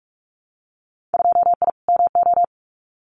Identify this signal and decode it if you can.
Morse code, heard as a keyed tone, about 46 wpm, 700 Hz. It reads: 2I MO
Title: Winter Field Day Exchanges 45wpm